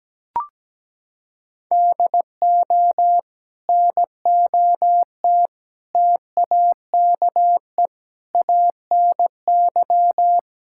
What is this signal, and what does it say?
Morse code, heard as a keyed tone, about 17 wpm, 700 Hz. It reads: DO NOT TAKE ANY
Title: Sentences from Top 1000 Words - Review Part 5 - 17wpm